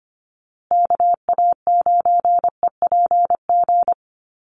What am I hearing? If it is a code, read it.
Morse code, heard as a keyed tone, about 25 wpm, 700 Hz. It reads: KA9EPG
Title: USA Call Signs Repeated 3x 25wpm